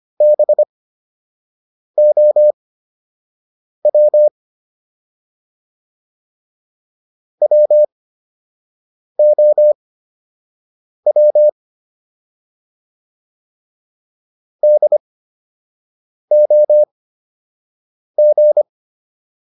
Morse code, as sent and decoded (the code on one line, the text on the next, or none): -... --- .-- / .-- --- .-- / -.. --- --.
BOW WOW DOG